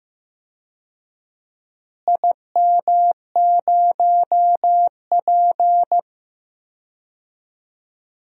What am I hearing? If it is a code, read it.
Morse code, heard as a keyed tone, about 15 wpm, 700 Hz. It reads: IM0P
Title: DXCC-100 Call Signs - 4 Characters - Repeated 3x - 15wpm